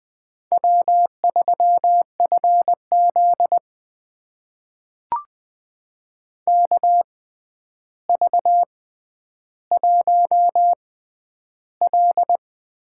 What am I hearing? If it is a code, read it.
Morse code, heard as a keyed tone, about 20 wpm, spaced as for 10 wpm, 700 Hz. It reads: W3FZ KV1L
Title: USA Call Signs - 4 Characters - 6x Character Spacing - 20wpm